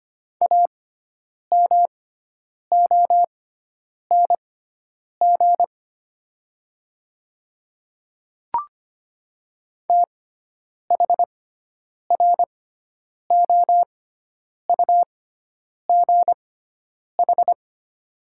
Morse code, heard as a keyed tone, about 25 wpm, spaced as for 9 wpm, 700 Hz. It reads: AMONG THROUGH
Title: 6x Character Spacing - Top 500 Words - 25wpm